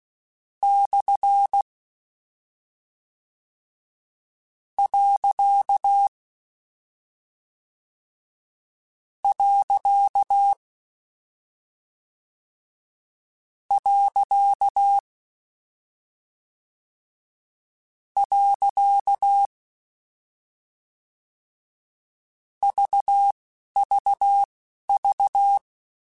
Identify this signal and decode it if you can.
Morse code, heard as a keyed tone, about 16 wpm, spaced as for 11 wpm, 800 Hz. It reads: / . . VVV
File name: Misc / Misc-mp3s / Punct @10